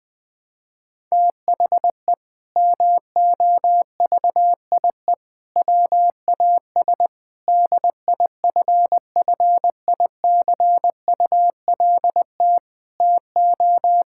Morse code, heard as a keyed tone, about 21 wpm, 700 Hz. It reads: THE MOVIE WAS DIFFICULT TO
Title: Sentences from Top 1000 Words - Review Part 5 - 20wpm